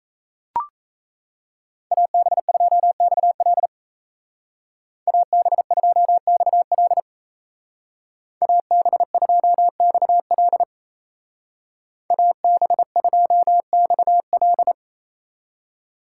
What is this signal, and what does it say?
Morse code, heard as a keyed tone, about 34 wpm, 700 Hz. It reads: AB2XL AB2XL AB2XL AB2XL
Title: Speed Racing - USA 2x2 Call Signs 28wpm